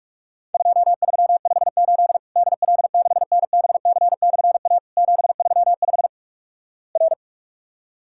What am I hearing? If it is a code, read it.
Morse code, heard as a keyed tone, about 46 wpm, 700 Hz. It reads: 2358 DL6NBC/A 735 R
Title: Worked All Europe DX Contest - QTC Exchange - 45wpm